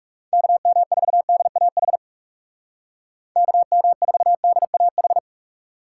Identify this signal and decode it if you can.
Morse code, heard as a keyed tone, about 37 wpm, 700 Hz. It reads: KM4DAH KM4DAH
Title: Speed Racing - USA 2x3 Call Signs 30wpm